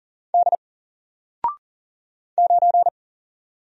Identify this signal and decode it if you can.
Morse code, heard as a keyed tone, about 40 wpm, 700 Hz. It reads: D 9
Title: Single Character (Letter:Number:Punctuation:Prosign) Alt 40wpm